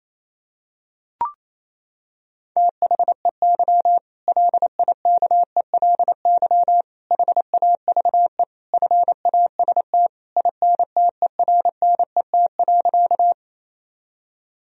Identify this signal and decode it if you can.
Morse code, heard as a keyed tone, about 28 wpm, 700 Hz. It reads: THEY LIKELY HAVE FAST INTERNET.
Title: Sentences from Top 1000 Words - Review Part 4 - 28wpm